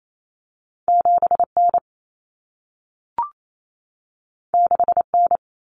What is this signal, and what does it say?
Morse code, heard as a keyed tone, about 28 wpm, 700 Hz. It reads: 7N 6N
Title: Common Cut Numbers - 2-Digit 28wpm